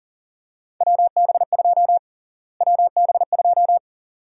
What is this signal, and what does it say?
Morse code, heard as a keyed tone, about 40 wpm, 700 Hz. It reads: WB2 WB2
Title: USA Call Sign Prefixes - Repeated 2x - 40wpm